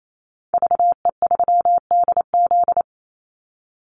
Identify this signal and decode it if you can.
Morse code, heard as a keyed tone, about 28 wpm, 700 Hz. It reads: VE3DZ
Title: Call Signs - 28wpm